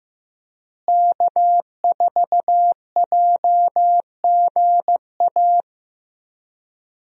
Morse code, heard as a keyed tone, about 15 wpm, 700 Hz. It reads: K4JGA
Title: USA Call Signs - 5 Characters - Repeated 3x - 15wpm